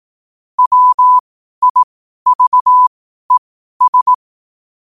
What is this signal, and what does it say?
Morse code, heard as a keyed tone, about 18 wpm, spaced as for 12 wpm, 1000 Hz. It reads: WIVES
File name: qotdfiles / 12wpm